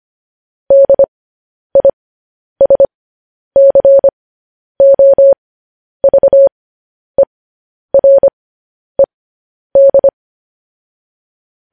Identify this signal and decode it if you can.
Morse code, heard as a keyed tone, about 25 wpm, spaced as for 10 wpm, 550 Hz. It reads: DISCOVERED
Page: news-headlines-in-morse-code-at-10-wpm